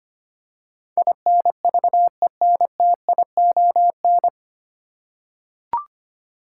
Morse code, heard as a keyed tone, about 25 wpm, 700 Hz. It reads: INVENTION
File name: practice / 25wpm / 9 Letter Words - First 3 Letters Given - 25wpm